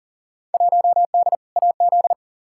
Morse code, heard as a keyed tone, about 37 wpm, 700 Hz. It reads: 1D AZ
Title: Field Day Exchanges - 40wpm